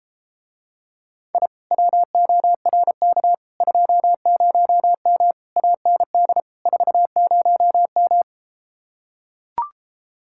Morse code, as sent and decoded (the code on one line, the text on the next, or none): .. / .-- --- .-. -.- / ..--- ----- -- / .- -. -.. / ....- ----- --
I WORK 20M AND 40M